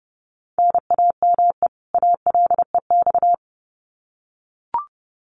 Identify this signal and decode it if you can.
Morse code, heard as a keyed tone, about 30 wpm, 700 Hz. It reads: NAME ALEX
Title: Common QSO Phrases 30wpm